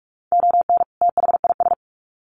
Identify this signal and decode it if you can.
Morse code, heard as a keyed tone, about 44 wpm, 700 Hz. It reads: ON THIS
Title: Sets of 2 Words 45wpm